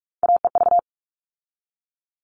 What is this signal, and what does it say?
Morse code, heard as a keyed tone, about 44 wpm, 700 Hz. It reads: AEV
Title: All Cut Numbers - 3-Digit 45wpm